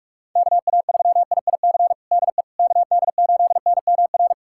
Morse code, heard as a keyed tone, about 46 wpm, 700 Hz. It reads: KA3IIC DE KD8NMR
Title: USA Callsign DE Callsign - 6 Characters - 45wpm